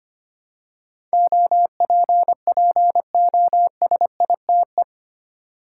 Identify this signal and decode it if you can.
Morse code, heard as a keyed tone, about 25 wpm, 700 Hz. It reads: OPPOSITE
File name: 8 Letter Words - First 3 Letters Given - 25wpm